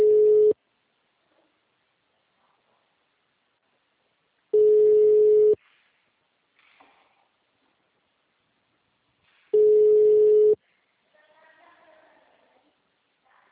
Ringback tone (Europe, 425 Hz).